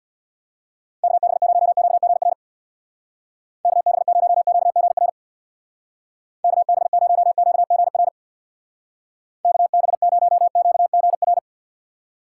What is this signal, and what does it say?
Morse code, heard as a keyed tone, about 62 wpm, 700 Hz. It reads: KB0QGR KB0QGR KB0QGR KB0QGR
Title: Speed Racing - USA Call Signs 50wpm